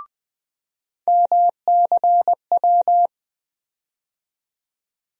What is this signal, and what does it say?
Morse code, heard as a keyed tone, about 20 wpm, 700 Hz. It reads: MCW
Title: USA Call Sign Postfixes - 3 Letters 20wpm